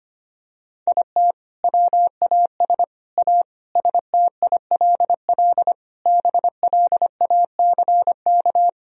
Morse code, heard as a keyed tone, about 25 wpm, 700 Hz. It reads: IT WAS A STILL BLACK
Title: Sentences from Top 1000 Words - Review Part 2 - 25wpm